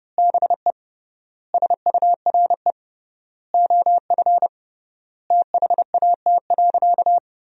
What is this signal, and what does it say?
Morse code, heard as a keyed tone, about 30 wpm, 700 Hz. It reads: BE SURE OF THAT.
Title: 3x Word-Spacing - Sentences from Top 300 Words - 30wpm